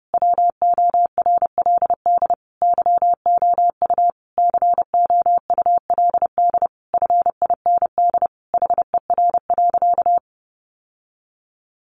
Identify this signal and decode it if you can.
Morse code, heard as a keyed tone, about 28 wpm, 700 Hz. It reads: WORLD YOU COULD FIND HER.
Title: Sentences from Top 1400 Words 30wpm